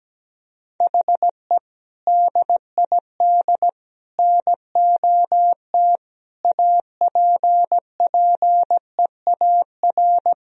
Morse code, heard as a keyed tone, about 17 wpm, 700 Hz. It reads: HE DID NOT APPEAR
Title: Sentences from Top 1000 Words - Review Part 2 - 17wpm